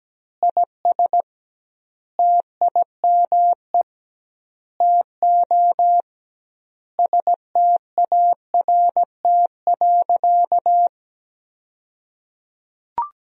Morse code, heard as a keyed tone, about 17 wpm, 700 Hz. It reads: IS TIME TO START.
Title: 2x Word-Spacing - Sentences from Top 300 Words - 17wpm